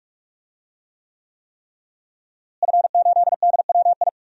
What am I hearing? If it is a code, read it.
Morse code, heard as a keyed tone, about 44 wpm, 700 Hz. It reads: F8DWI
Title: DXCC-100 Call Signs - 5 Characters - Repeated 3x - 45wpm